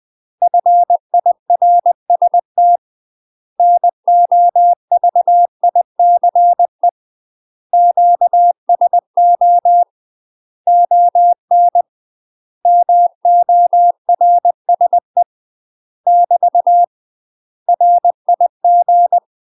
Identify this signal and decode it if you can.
Morse code, heard as a keyed tone, about 20 wpm, 700 Hz. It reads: FIRST NOVICE QSO ON MORSE <BT> RIG
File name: UK MASTER QSOs TRAINING / 20WPM / UK MASTER QSOs 0008